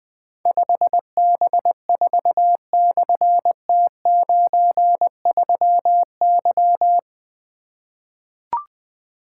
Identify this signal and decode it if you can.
Morse code, heard as a keyed tone, about 20 wpm, 700 Hz. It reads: 5B4/T93Y
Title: Difficult Call Signs 20wpm